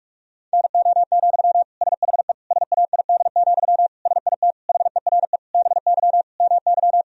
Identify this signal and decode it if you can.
Morse code, heard as a keyed tone, about 46 wpm, 700 Hz. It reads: NO, SHE SAID, SIT HERE BY MY
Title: Sentences from Top 1000 Words - Review Part 1 - 45wpm